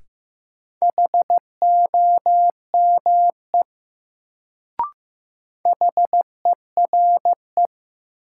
Morse code, heard as a keyed tone, about 15 wpm, 700 Hz. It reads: HOME HERE
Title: Sentences from Top 200 Words - New Word Drill - 15wpm